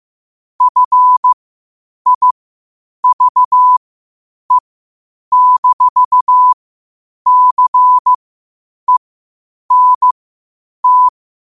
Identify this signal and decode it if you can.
Morse code, heard as a keyed tone, about 15 wpm, spaced as for 8 wpm, 1000 Hz. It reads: FIVE-CENT
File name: qotdfiles / 8wpm